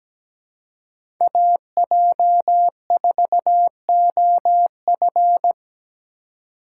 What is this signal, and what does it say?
Morse code, heard as a keyed tone, about 17 wpm, 700 Hz. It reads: AJ4OF
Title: USA Call Signs - 5 Characters - 3x Character Spacing - 17wpm